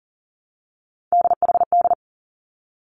Morse code, heard as a keyed tone, about 38 wpm, 700 Hz. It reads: DHD
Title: Three Challenging Letter-Number Combinations 40wpm